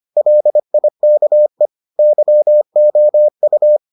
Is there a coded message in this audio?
Yes: Morse code from 0.2 to 3.8 s, heard as a keyed tone, about 25 wpm, 600 Hz. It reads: LIKE YOU